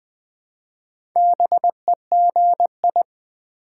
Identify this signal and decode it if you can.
Morse code, heard as a keyed tone, about 20 wpm, 700 Hz. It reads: BEGI
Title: Build A Word - 1 Letter At A Time - 3 to 5-Letter Words - 20wpm